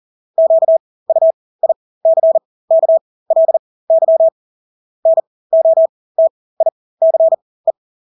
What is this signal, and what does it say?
Morse code, heard as a keyed tone, about 40 wpm, spaced as for 20 wpm, 650 Hz. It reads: QUICKLY NOTICE
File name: Engelske ord 40-20 wpm